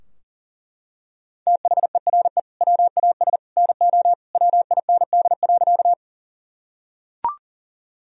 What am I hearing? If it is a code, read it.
Morse code, heard as a keyed tone, about 40 wpm, 700 Hz. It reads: THERE WAS NO WIND.
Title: Sentences from Top 1000 Words - Review Part 2 - 40wpm